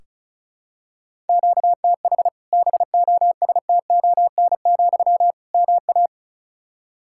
Morse code, heard as a keyed tone, about 36 wpm, 700 Hz. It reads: QTH BOSTON, MA
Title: Practice QSO - Exchange 1 - Beginning - 35wpm